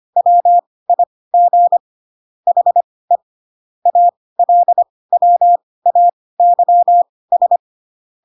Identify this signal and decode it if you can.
Morse code, heard as a keyed tone, about 25 wpm, spaced as for 18 wpm, 700 Hz. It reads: WIG HE ALWAYS
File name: Pinocchio / Pinocchio ch2 12 25 ik1hge